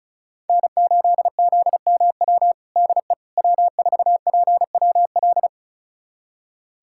Morse code, heard as a keyed tone, about 36 wpm, 700 Hz. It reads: N8ZMW DE W4PWL
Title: USA Callsign DE Callsign - 5 Characters - 35wpm